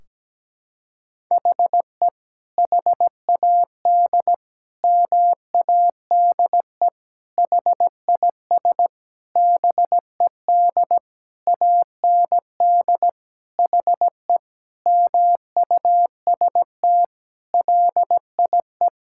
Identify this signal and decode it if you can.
Morse code, heard as a keyed tone, about 17 wpm, 700 Hz. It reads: HE HAD MADE HIS BED AND HE MUST LIE
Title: Sentences from Top 1600 Words 17wpm